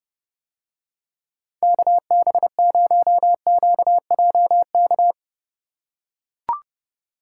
Morse code, heard as a keyed tone, about 30 wpm, 700 Hz. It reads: KB0QJK